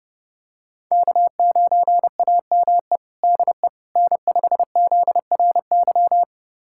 Morse code, heard as a keyed tone, about 30 wpm, 700 Hz. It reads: K9AME DE N5ZRY
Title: USA Callsign DE Callsign - 5 Characters - 30wpm